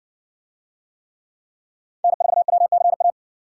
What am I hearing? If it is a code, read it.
Morse code, heard as a keyed tone, about 57 wpm, 700 Hz. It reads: N4WCA